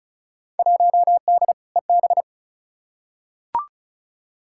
Morse code, heard as a keyed tone, about 36 wpm, 700 Hz. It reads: 1D EB
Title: Field Day Exchanges - 35wpm